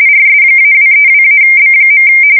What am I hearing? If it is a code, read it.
RTTY, about 45.45 baud. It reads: RS-232-B.
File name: fsk170 / teletype-corp